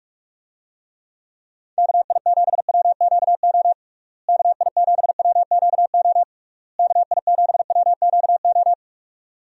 Morse code, heard as a keyed tone, about 46 wpm, 700 Hz. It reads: KI7WQO KI7WQO KI7WQO
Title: USA Call Signs - 6 Characters - Repeated 3x - 45wpm